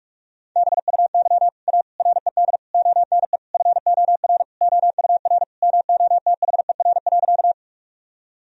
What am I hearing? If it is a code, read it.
Morse code, heard as a keyed tone, about 44 wpm, 700 Hz. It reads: BUY A RED ONE FOR OUR MOTHER.
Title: Sentences from Top 1000 Words - Review Part 2 - 45wpm